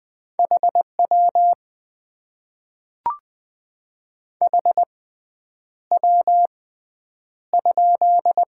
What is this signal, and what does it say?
Morse code, heard as a keyed tone, about 20 wpm, spaced as for 9 wpm, 700 Hz. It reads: HW HW?
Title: Top 90 QSO Elements - 6x Character Spacing - 20wpm